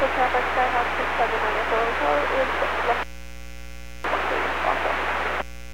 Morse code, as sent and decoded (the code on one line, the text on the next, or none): none